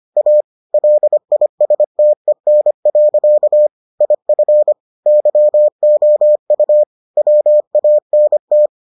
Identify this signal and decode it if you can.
Morse code, heard as a keyed tone, about 25 wpm, 600 Hz. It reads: A LISTEN. IF YOU WANT